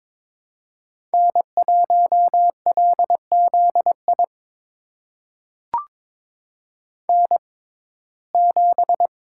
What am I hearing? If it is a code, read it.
Morse code, heard as a keyed tone, about 22 wpm, spaced as for 16 wpm, 700 Hz. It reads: N1LZI N7
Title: USA Call Signs - 5 Characters - 6x Character Spacing - 22wpm